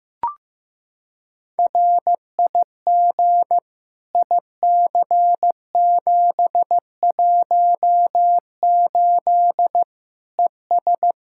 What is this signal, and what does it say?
Morse code, heard as a keyed tone, about 15 wpm, 700 Hz. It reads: RIG IC718 ES
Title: Practice QSO - Exchange 2 - Info - 15wpm